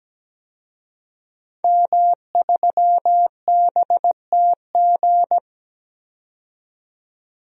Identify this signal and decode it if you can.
Morse code, heard as a keyed tone, about 17 wpm, 700 Hz. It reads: M3BTG
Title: DXCC-100 Call Signs - 5 Characters - 17wpm